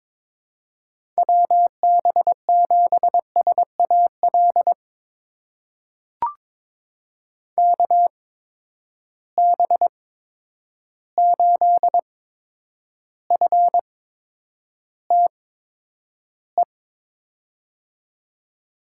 Morse code, heard as a keyed tone, about 21 wpm, spaced as for 9 wpm, 700 Hz. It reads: WB7SAL KB8FTE
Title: USA Call Signs - 6 Characters - 8x Character Spacing - 22wpm